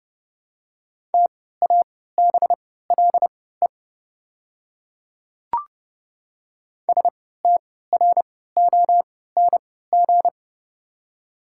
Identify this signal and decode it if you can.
Morse code, heard as a keyed tone, about 32 wpm, spaced as for 17 wpm, 700 Hz. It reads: TABLE STRONG
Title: 3x Character Spacing - Top 500 Words - 30wpm